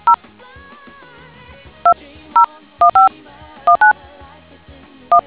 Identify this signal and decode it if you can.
DTMF, heard as telephone touch tones, tones of about 100 ms. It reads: 0 201519 1